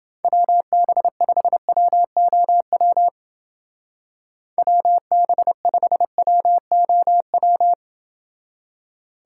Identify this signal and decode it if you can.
Morse code, heard as a keyed tone, about 28 wpm, 700 Hz. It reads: WB5WOW WB5WOW